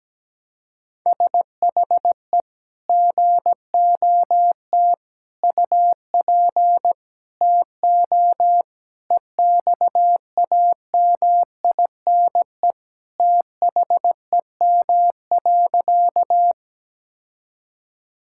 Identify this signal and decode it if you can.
Morse code, heard as a keyed tone, about 17 wpm, 700 Hz. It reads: SHE GOT UP TO EXAMINE THEM.
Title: Sentences from Top 1600 Words 17wpm